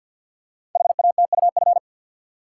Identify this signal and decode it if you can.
Morse code, heard as a keyed tone, about 52 wpm, 700 Hz. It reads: HATUF
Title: Code Groups - 5 Letters - 50wpm